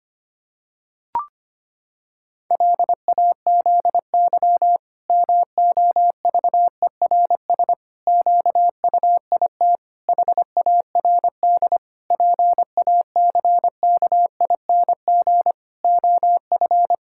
Morse code, heard as a keyed tone, about 25 wpm, 700 Hz. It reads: LAZY MOVERS QUIT HARD PACKING OF
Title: Pangram Challenge v2 - 25wpm